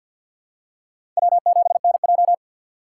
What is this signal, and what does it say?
Morse code, heard as a keyed tone, about 53 wpm, 700 Hz. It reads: W7NJ
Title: USA Call Signs - 4 Characters - 3x Character Spacing - 50wpm